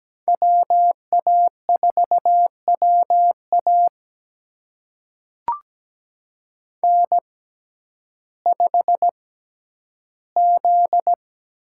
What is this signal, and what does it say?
Morse code, heard as a keyed tone, about 17 wpm, spaced as for 10 wpm, 700 Hz. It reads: WA4WA N5Z
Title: USA Call Signs - 6x Character Spacing - 17wpm